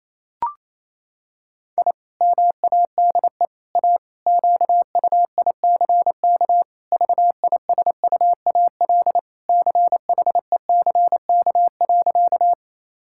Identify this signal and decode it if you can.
Morse code, heard as a keyed tone, about 28 wpm, 700 Hz. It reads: I MADE A QUICK VISUAL CHECK.
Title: Sentences from Top 1000 Words - Review Part 6 - 28wpm